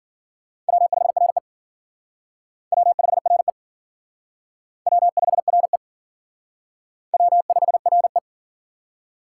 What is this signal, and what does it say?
Morse code, heard as a keyed tone, about 51 wpm, 700 Hz. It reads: W5RE W5RE W5RE W5RE